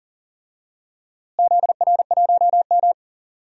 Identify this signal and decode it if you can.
Morse code, heard as a keyed tone, about 40 wpm, 700 Hz. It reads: ZR1M
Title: DXCC-100 Call Signs - 4 Characters - Repeated 3x - 40wpm